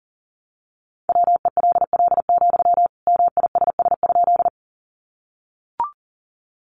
Morse code, heard as a keyed tone, about 40 wpm, 700 Hz. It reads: WELL, MISS?
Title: Sentences from Top 500 Words 40wpm